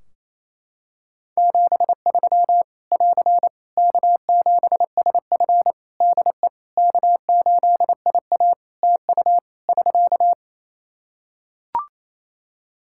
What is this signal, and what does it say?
Morse code, heard as a keyed tone, about 27 wpm, 700 Hz. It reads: 73 <AR> K7SF DE K8IA TU <SK>
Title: Practice QSO - Exchange 4 - Ending - 28wpm